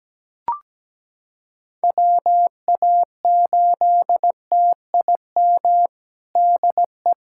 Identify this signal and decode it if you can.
Morse code, heard as a keyed tone, about 17 wpm, 700 Hz. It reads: WA8TIM DE
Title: USA Callsign DE Callsign 17wpm